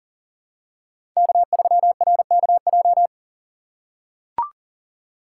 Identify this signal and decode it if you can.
Morse code, heard as a keyed tone, about 40 wpm, 700 Hz. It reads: K3RKJ
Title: USA Call Signs - 5 Characters - 4x Character Spacing - 40wpm